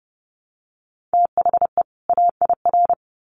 Morse code, heard as a keyed tone, about 30 wpm, 700 Hz. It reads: THE AIR
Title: Sets of 2 Words 30wpm